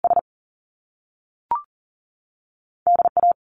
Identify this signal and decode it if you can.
Morse code, heard as a keyed tone, about 39 wpm, 700 Hz. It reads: S DA
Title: Top Two Letter Combinations in Words 40wpm